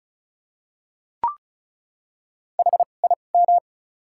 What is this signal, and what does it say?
Morse code, heard as a keyed tone, about 34 wpm, spaced as for 25 wpm, 700 Hz. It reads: HIM
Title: 2x Character Spacing - Top 300 Words - 35wpm